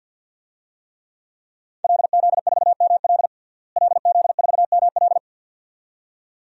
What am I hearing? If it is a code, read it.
Morse code, heard as a keyed tone, about 53 wpm, 700 Hz. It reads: LZ4ML LZ4ML
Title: DXCC-100 Call Signs - 5 Characters - Repeated 2x - 50wpm